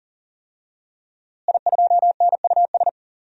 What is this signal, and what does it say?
Morse code, heard as a keyed tone, about 40 wpm, 700 Hz. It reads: I2NUS
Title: DXCC-100 Call Signs - 5 Characters - Repeated 2x - 40wpm